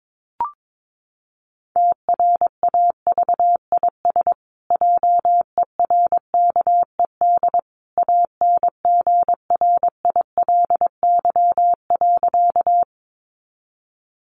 Morse code, heard as a keyed tone, about 22 wpm, 700 Hz. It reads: TRAVIS JERKED ANGRILY.
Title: A Sound Of Thunder 22wpm